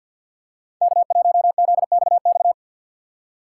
Morse code, heard as a keyed tone, about 53 wpm, 700 Hz. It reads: K1ZXX
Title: USA Call Signs - 5 Characters - 8x Character Spacing - 50wpm